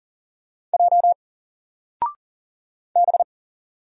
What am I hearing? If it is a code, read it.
Morse code, heard as a keyed tone, about 40 wpm, 700 Hz. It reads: J B
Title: Single Letter-Number 40wpm